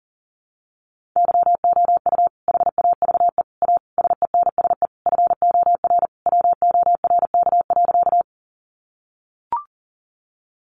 Morse code, heard as a keyed tone, about 40 wpm, 700 Hz. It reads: YOU HAVE A SENSE FOR WORK.
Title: Sentences from Top 700 Words 40wpm